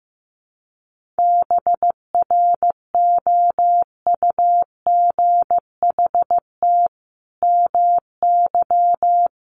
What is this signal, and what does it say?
Morse code, heard as a keyed tone, about 15 wpm, 700 Hz. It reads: BROUGHT MY
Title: Sentences from Top 700 Words 15wpm